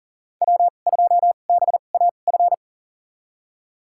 Morse code, heard as a keyed tone, about 40 wpm, spaced as for 29 wpm, 700 Hz. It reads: W2BAF
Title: USA Call Signs - 5 Characters - 2x Character Spacing - 40wpm